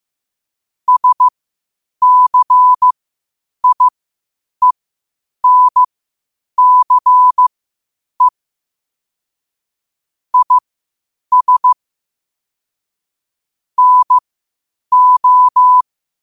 Morse code, heard as a keyed tone, about 15 wpm, spaced as for 8 wpm, 1000 Hz. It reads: SCIENCE IS NO
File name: qotdfiles / 8wpm